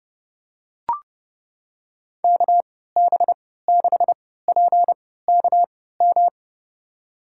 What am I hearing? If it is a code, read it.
Morse code, heard as a keyed tone, about 31 wpm, spaced as for 17 wpm, 700 Hz. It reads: KB6PKM
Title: USA Call Signs - 3x Character Spacing - 30wpm